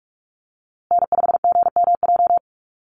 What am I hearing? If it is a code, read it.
Morse code, heard as a keyed tone, about 44 wpm, 700 Hz. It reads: N5GMJ
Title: USA Call Signs Repeated 2x 45wpm